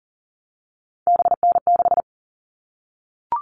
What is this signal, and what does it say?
Morse code, heard as a keyed tone, about 40 wpm, 700 Hz. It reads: BN6
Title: All Cut Numbers - 3-Digit 40wpm